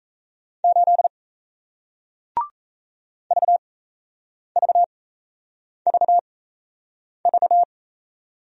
Morse code, heard as a keyed tone, about 38 wpm, spaced as for 8 wpm, 700 Hz. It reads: 8 VVVV